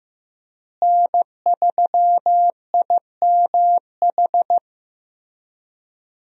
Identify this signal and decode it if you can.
Morse code, heard as a keyed tone, about 15 wpm, 700 Hz. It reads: N3IMH
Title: USA Call Signs - 5 Characters - Repeated 2x - 15wpm